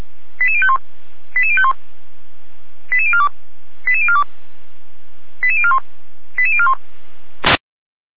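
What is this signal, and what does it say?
Five-tone selective call (ZVEI), heard as quick runs of five tones, tones of about 80 ms. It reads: 88051 88051 88042 88042 88041 88041
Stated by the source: ZVEI five-tone selective call received by SDR, from the feuerwehr-orlamuende.org (fire brigade Orlamünde) server.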